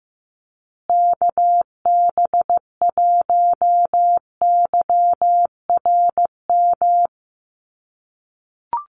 Morse code, heard as a keyed tone, about 15 wpm, 700 Hz. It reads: KB1YRM
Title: USA Call Signs Repeated 2x 15wpm